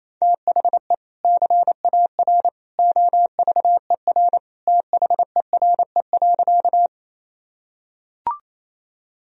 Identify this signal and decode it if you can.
Morse code, heard as a keyed tone, about 27 wpm, 700 Hz. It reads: THE CAR OVER THERE.
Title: Sentences from Top 1000 Words - Review Part 1 - 28wpm